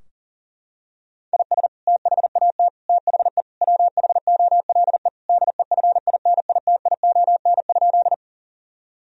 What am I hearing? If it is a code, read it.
Morse code, heard as a keyed tone, about 40 wpm, 700 Hz. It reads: IS THAT THE WHOLE DEFINITION?
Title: Sentences from Top 1000 Words - Review Part 4 - 40wpm